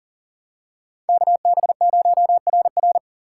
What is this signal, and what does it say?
Morse code, heard as a keyed tone, about 39 wpm, 700 Hz. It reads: KB0RR
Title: USA Call Signs - 2x Character Spacing - 40wpm